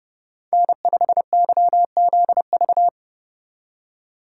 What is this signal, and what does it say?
Morse code, heard as a keyed tone, about 34 wpm, 700 Hz. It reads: N5YZV
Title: Speed Racing - USA 1x3 Call Signs 30wpm